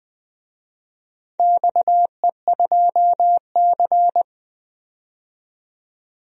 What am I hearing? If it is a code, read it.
Morse code, heard as a keyed tone, about 20 wpm, 700 Hz. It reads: XE2C
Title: DXCC-100 Call Signs - 4 Characters - 20wpm